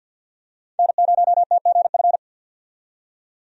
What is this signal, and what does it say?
Morse code, heard as a keyed tone, about 53 wpm, 700 Hz. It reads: N0TGF
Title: USA Call Signs - 5 Characters - 50wpm